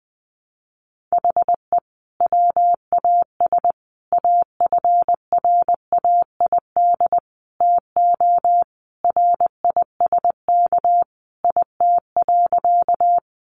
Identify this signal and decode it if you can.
Morse code, heard as a keyed tone, about 19 wpm, 700 Hz. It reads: HE WAS AFRAID TO RISK IT.
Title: Sentences from Top 1500 Words 20wpm